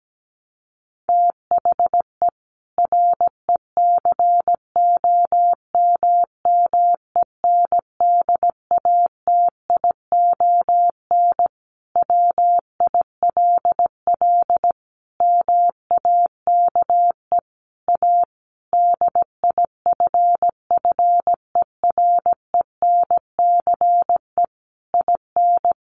Morse code, heard as a keyed tone, about 17 wpm, 700 Hz. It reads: THE RECOMMENDATION WILL MAKE A DIFFERENCE IN
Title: Sentences from Top 1400 Words 17wpm